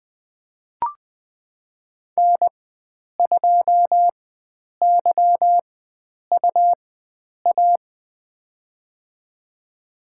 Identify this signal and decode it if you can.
Morse code, heard as a keyed tone, about 21 wpm, spaced as for 9 wpm, 700 Hz. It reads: N2YUA
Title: USA Call Signs - 5 Characters - 4x Character Spacing - 20wpm